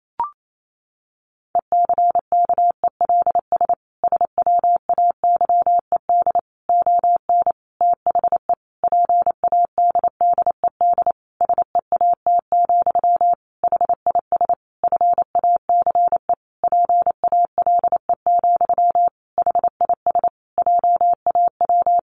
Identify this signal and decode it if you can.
Morse code, heard as a keyed tone, about 28 wpm, 700 Hz. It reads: ECKELS SWAYED ON THE PADDED SEAT, HIS FACE PALE, HIS JAW
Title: A Sound Of Thunder 28wpm